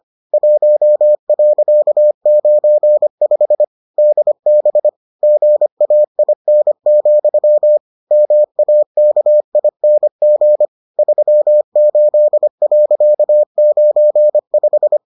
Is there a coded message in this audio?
Yes: Morse code from 0.3 to 15.0 s, heard as a keyed tone, about 25 wpm, 600 Hz. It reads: 1.95 DB GAIN, MAKING 38.95